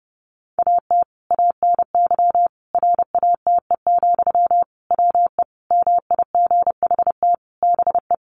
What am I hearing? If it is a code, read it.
Morse code, heard as a keyed tone, about 30 wpm, 700 Hz. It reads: AT ANY RATE, WE MIGHT BE
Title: Sentences from Top 700 Words 30wpm